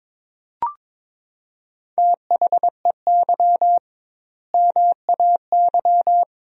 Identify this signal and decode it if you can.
Morse code, heard as a keyed tone, about 22 wpm, 700 Hz. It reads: THEY MAY
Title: 2x Word-Spacing - Sentences from Top 300 Words - 22wpm